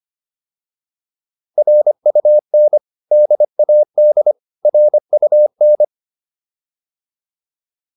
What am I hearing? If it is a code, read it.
Morse code, heard as a keyed tone, about 25 wpm, 600 Hz. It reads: RUN DAD RUN